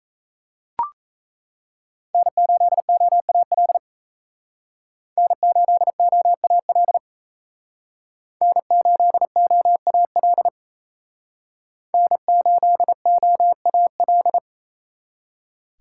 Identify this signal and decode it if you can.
Morse code, heard as a keyed tone, about 35 wpm, 700 Hz. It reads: N8OAL N8OAL N8OAL N8OAL
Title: Speed Racing - USA 1x3 Call Signs 28wpm